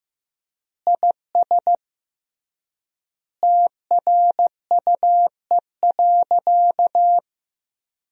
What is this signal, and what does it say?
Morse code, heard as a keyed tone, about 15 wpm, 700 Hz. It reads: IS TRUE.
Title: 3x Word-Spacing - Sentences from Top 500 Words - 15wpm